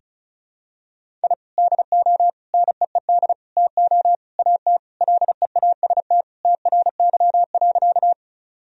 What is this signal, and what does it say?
Morse code, heard as a keyed tone, about 36 wpm, 700 Hz. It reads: I DO NEED TO AT LEAST TRY.
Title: Sentences from Top 1000 Words - Review Part 1 - 35wpm